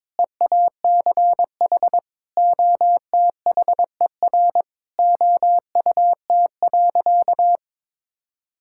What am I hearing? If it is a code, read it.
Morse code, heard as a keyed tone, about 22 wpm, 700 Hz. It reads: EACH OTHER OUT.
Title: Sentences from Top 1000 Words - Review Part 6 - 22wpm